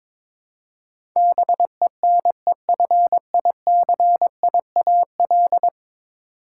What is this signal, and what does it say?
Morse code, heard as a keyed tone, about 22 wpm, 700 Hz. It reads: BENEFICIAL
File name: practice / 22wpm / 10 Letter Words - First 3 Letters Given - 22wpm